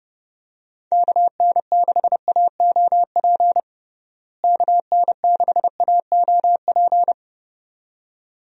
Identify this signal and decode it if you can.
Morse code, heard as a keyed tone, about 32 wpm, 700 Hz. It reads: KN6AOP KN6AOP
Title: USA Call Signs - 6 Characters - Repeated 2x - 30wpm